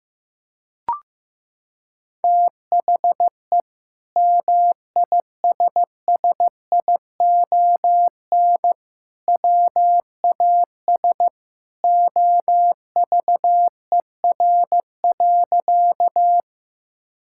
Morse code, heard as a keyed tone, about 15 wpm, 700 Hz. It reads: THE MISSION WAS OVER.
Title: Sentences from Top 1600 Words 15wpm